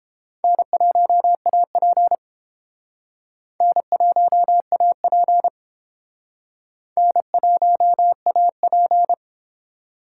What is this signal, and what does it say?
Morse code, heard as a keyed tone, about 30 wpm, 700 Hz. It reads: N1AP N1AP N1AP